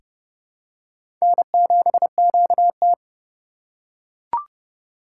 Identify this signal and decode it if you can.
Morse code, heard as a keyed tone, about 30 wpm, 700 Hz. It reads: N7QT
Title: K1USN SST QSO Practice - 30wpm